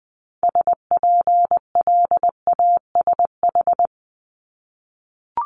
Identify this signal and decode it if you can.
Morse code, heard as a keyed tone, about 20 wpm, 700 Hz. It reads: SPLASH
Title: Multi-letter Phonemes Repeated 3x 20wpm